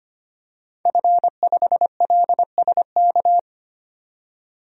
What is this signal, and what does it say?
Morse code, heard as a keyed tone, about 25 wpm, 700 Hz. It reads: F5LSK
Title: DXCC-100 Call Signs - 5 Characters - 25wpm